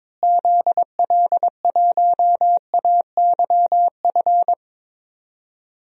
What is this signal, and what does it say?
Morse code, heard as a keyed tone, about 22 wpm, 700 Hz. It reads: ZL1AYF
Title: DXCC-100 Call Signs - 6 Characters - Repeated 3x - 22wpm